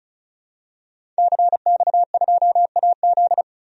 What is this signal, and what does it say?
Morse code, heard as a keyed tone, about 34 wpm, 700 Hz. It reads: CX2AZ
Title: DXCC-100 Call Signs - 5 Characters - 35wpm